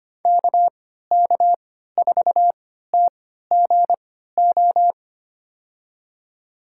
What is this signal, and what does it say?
Morse code, heard as a keyed tone, about 25 wpm, spaced as for 14 wpm, 700 Hz. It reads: KK4TGO
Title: USA Call Signs - 6 Characters - 3x Character Spacing - 25wpm